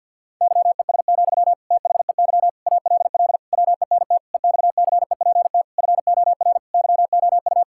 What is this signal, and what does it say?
Morse code, heard as a keyed tone, about 50 wpm, 700 Hz. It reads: YES, THEY ALL WENT EXCEPT FOR YOU
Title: Sentences from Top 1600 Words 50wpm